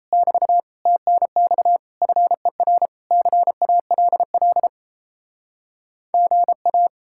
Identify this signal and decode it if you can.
Morse code, heard as a keyed tone, about 37 wpm, 700 Hz. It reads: <BT> TNX FER CALL GA